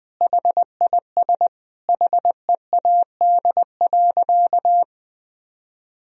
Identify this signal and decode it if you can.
Morse code, heard as a keyed tone, about 21 wpm, 700 Hz. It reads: HIS HEAD.
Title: Sentences from Top 1000 Words - Review Part 4 - 20wpm